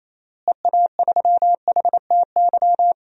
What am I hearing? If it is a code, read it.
Morse code, heard as a keyed tone, about 27 wpm, 700 Hz. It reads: EA3HTY
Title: DXCC-100 Call Signs - 6 Characters - Repeated 2x - 28wpm